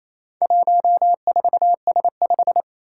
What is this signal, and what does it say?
Morse code, heard as a keyed tone, about 27 wpm, 700 Hz. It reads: 14S5
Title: Four Challenging Letter-Number Combinations 28wpm